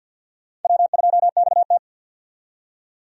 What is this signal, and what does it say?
Morse code, heard as a keyed tone, about 50 wpm, 700 Hz. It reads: W2XT
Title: USA Call Signs - 4 Characters - 50wpm